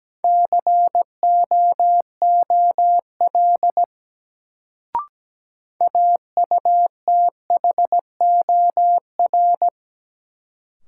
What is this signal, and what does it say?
Morse code, heard as a keyed tone, about 17 wpm, 700 Hz. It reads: COOL AUTHOR
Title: Sentences from Top 1200 Words - New Word Drill Alt - 17wpm